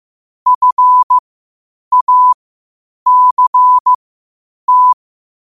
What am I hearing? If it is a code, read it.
Morse code, heard as a keyed tone, about 15 wpm, spaced as for 8 wpm, 1000 Hz. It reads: FACT
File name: qotdfiles / 8wpm